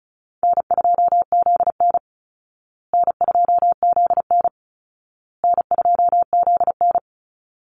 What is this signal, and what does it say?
Morse code, heard as a keyed tone, about 34 wpm, 700 Hz. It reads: N2ZN N2ZN N2ZN
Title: Easy Call Signs Repeated 3x - 35wpm